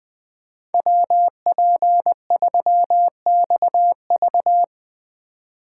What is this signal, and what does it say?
Morse code, heard as a keyed tone, about 19 wpm, 700 Hz. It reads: WP3XV
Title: Speed Racing - USA 2x2 Call Signs 20wpm